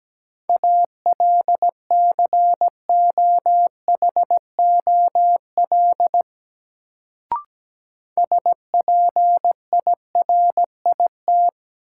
Morse code, heard as a keyed tone, about 17 wpm, 700 Hz. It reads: ALCOHOL SPIRIT
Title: Sentences from Top 1400 Words - New Word Drill - 17wpm